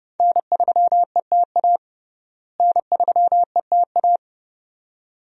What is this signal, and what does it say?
Morse code, heard as a keyed tone, about 32 wpm, 700 Hz. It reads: N3ETA N3ETA
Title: USA Call Signs - 5 Characters - Repeated 2x - 30wpm